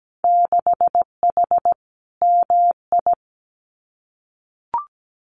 Morse code, heard as a keyed tone, about 17 wpm, 700 Hz. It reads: 6H MI
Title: Winter Field Day Exchanges Encore 17wpm